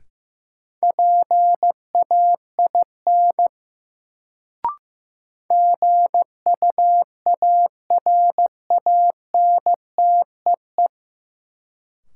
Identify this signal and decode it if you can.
Morse code, heard as a keyed tone, about 15 wpm, 700 Hz. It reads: PAIN GUARANTEE
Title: Sentences from Top 1600 Words - New Word Drill Alt - 15wpm